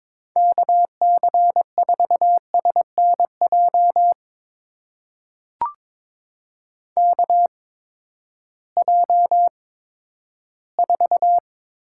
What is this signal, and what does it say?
Morse code, heard as a keyed tone, about 22 wpm, spaced as for 12 wpm, 700 Hz. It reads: KC4SNJ KJ4
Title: USA Call Signs - 6 Characters - 8x Character Spacing - 22wpm